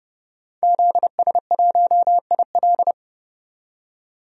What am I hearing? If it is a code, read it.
Morse code, heard as a keyed tone, about 32 wpm, 700 Hz. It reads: ZS1IL
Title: DXCC-100 Call Signs - 5 Characters - 30wpm